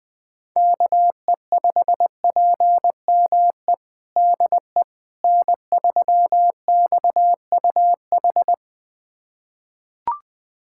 Morse code, heard as a keyed tone, about 21 wpm, 700 Hz. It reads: KE5PME DE N3XUH
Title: USA Callsign DE Callsign 20wpm